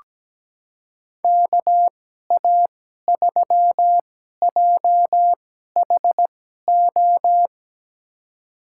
Morse code, heard as a keyed tone, about 17 wpm, spaced as for 12 wpm, 700 Hz. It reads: KA3JHO
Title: USA Call Signs - 6 Characters - 2x Character Spacing - 17wpm